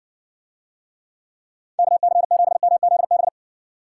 Morse code, heard as a keyed tone, about 61 wpm, 700 Hz. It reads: KC7MZB